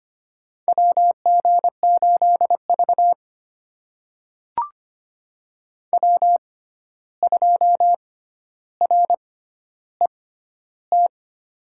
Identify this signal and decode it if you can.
Morse code, heard as a keyed tone, about 25 wpm, spaced as for 12 wpm, 700 Hz. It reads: WG8V W2RET
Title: USA Call Signs - 6x Character Spacing - 25wpm